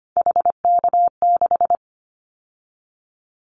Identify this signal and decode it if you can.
Morse code, heard as a keyed tone, about 25 wpm, 700 Hz. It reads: HK6
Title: Callsign Prefixes for Top 100 DXCC - 25wpm